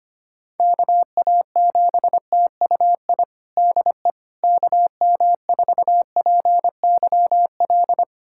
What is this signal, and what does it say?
Morse code, heard as a keyed tone, about 25 wpm, 700 Hz. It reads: KA7TUI DE KM4PYL
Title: USA Callsign DE Callsign 25wpm